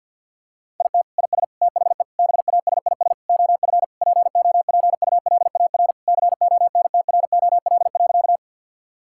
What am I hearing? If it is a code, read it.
Morse code, heard as a keyed tone, about 49 wpm, 690 Hz. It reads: IT IS THE BASIS OF POPULAR CONTROL.
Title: Sentences from Top 1000 Words - Review Part 5 - 50wpm